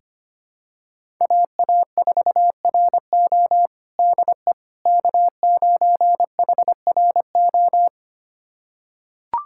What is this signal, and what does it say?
Morse code, heard as a keyed tone, about 25 wpm, 700 Hz. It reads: AA4RO DE K9HRO
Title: USA Callsign DE Callsign - 5 Characters - 25wpm